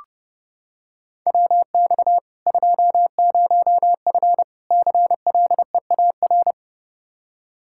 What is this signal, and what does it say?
Morse code, heard as a keyed tone, about 32 wpm, 700 Hz. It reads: WX 20F CLEAR
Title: K1USN SST QSO Practice 30wpm